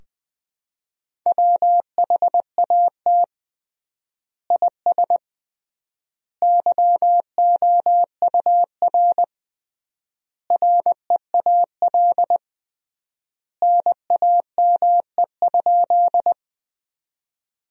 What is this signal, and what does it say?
Morse code, heard as a keyed tone, about 19 wpm, 700 Hz. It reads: WHAT IS YOUR REAL NAME?
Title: 3x Word-Spacing - Sentences from Top 300 Words - 20wpm